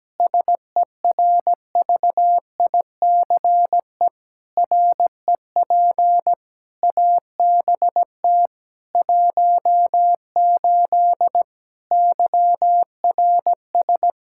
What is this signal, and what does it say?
Morse code, heard as a keyed tone, about 17 wpm, 700 Hz. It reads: SERVICE REP ABT 18 YRS
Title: Practice QSO - Exchange 3 - More Info - 17wpm